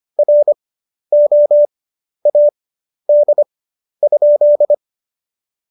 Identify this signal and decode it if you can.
Morse code, heard as a keyed tone, about 25 wpm, spaced as for 11 wpm, 590 Hz. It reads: ROAD?
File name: Jokes-25-char-10-eff